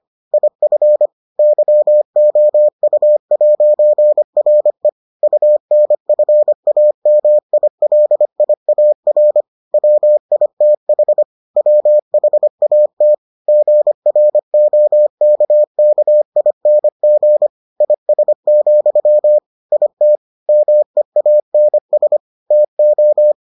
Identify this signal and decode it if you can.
Morse code, heard as a keyed tone, about 25 wpm, 600 Hz. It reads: IF YOU'RE UNFAMILIAR WITH WHAT GROKKING IS, IT MEANS TO